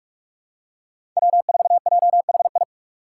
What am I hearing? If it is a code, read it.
Morse code, heard as a keyed tone, about 43 wpm, 700 Hz. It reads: W4JHI
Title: Five Challenging Letter-Number Combinations 45wpm